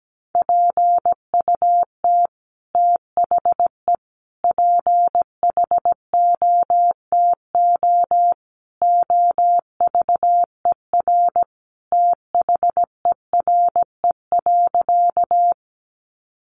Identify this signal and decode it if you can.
Morse code, heard as a keyed tone, about 17 wpm, 700 Hz. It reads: PUT THE PHOTO OVER THERE.
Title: Sentences from Top 1200 Words 17wpm